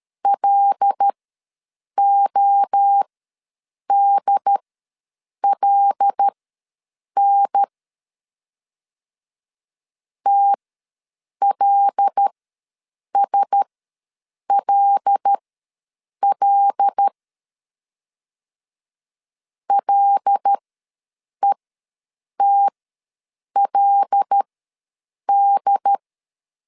Morse code, heard as a keyed tone, about 13 wpm, spaced as for 7 wpm, 800 Hz. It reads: LODLN TLSLL LETLD
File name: LEZIONI / 5 LBVC / 5 L ws 12wpm 2958